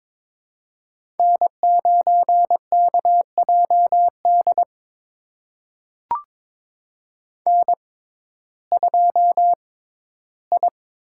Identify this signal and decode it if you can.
Morse code, heard as a keyed tone, about 22 wpm, spaced as for 14 wpm, 700 Hz. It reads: N9KJD N2I
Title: USA Call Signs - 6x Character Spacing - 22wpm